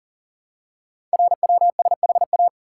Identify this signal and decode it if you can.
Morse code, heard as a keyed tone, about 43 wpm, 700 Hz. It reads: RWSHA
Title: Code Groups - 5 Letters - 40wpm